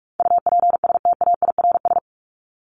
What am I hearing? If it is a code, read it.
Morse code, heard as a keyed tone, about 44 wpm, 700 Hz. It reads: UPSTAIRS
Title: Top 1751-2000 Words 45wpm